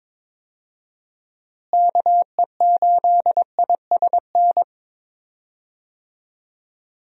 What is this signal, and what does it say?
Morse code, heard as a keyed tone, about 21 wpm, 700 Hz. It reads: KE8ISN
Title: USA Call Signs - 6 Characters - Repeated 2x - 22wpm